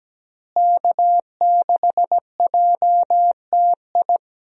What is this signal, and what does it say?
Morse code, heard as a keyed tone, about 17 wpm, 700 Hz. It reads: K6JTI
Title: USA Call Signs - 5 Characters - Repeated 3x - 17wpm